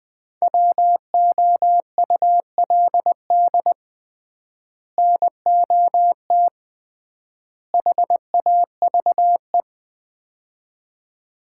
Morse code, heard as a keyed tone, about 20 wpm, 700 Hz. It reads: WOULD NOT HAVE
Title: 3x Word-Spacing - Sets of 3 Words - 20wpm